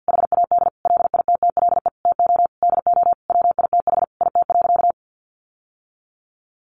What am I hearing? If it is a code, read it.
Morse code, heard as a keyed tone, about 50 wpm, 700 Hz. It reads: HAD LITTLE TO DO WITH IT.
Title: Sentences from Top 1500 Words 50wpm